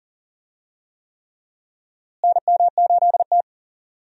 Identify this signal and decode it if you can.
Morse code, heard as a keyed tone, about 40 wpm, 700 Hz. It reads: NM8T
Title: USA Call Signs - 4 Characters - Repeated 3x - 40wpm